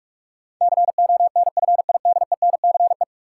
Morse code, heard as a keyed tone, about 43 wpm, 700 Hz. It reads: CONFIDENCE
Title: Sentences from Top 1500 Words - New Word Drill Alt - 45wpm